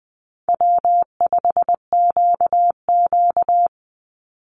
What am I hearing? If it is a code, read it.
Morse code, heard as a keyed tone, about 20 wpm, 700 Hz. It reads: W5QQ
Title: Call Signs Repeated 2x - 20wpm